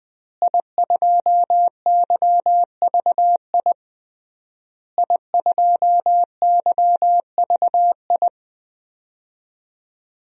Morse code, heard as a keyed tone, about 20 wpm, 700 Hz. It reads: I2YVI I2YVI
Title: DXCC-100 Call Signs - 5 Characters - Repeated 3x - 20wpm